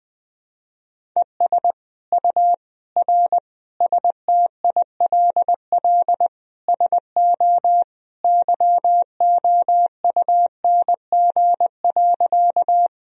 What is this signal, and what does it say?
Morse code, heard as a keyed tone, about 20 wpm, 700 Hz. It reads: ES U R STILL SO YOUNG.
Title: Sentences from Top 400 Words and Abbreviations 20wpm